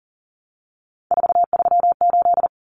Morse code, heard as a keyed tone, about 40 wpm, 700 Hz. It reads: 438
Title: Common Cut Numbers - 3-Digit 40wpm